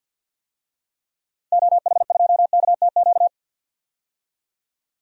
Morse code, heard as a keyed tone, about 51 wpm, 700 Hz. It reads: OH2KTQ
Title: DXCC-100 Call Signs - 6 Characters - 50wpm